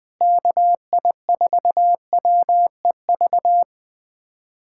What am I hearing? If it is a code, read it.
Morse code, heard as a keyed tone, about 21 wpm, 700 Hz. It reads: KI4WEV
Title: USA Callsign DE Callsign - 6 Characters - 20wpm